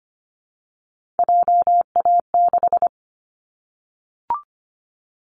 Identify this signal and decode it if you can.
Morse code, heard as a keyed tone, about 25 wpm, 700 Hz. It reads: JA6
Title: Callsign Prefixes for Top 25 DXCC - Repeated 3x - 25wpm